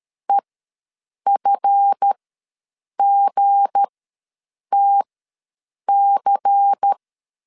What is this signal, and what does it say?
Morse code, heard as a keyed tone, about 13 wpm, spaced as for 7 wpm, 800 Hz. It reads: EFGTC
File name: LEZIONI / 7 GF9Z / 7 GF ws 12wpm 3140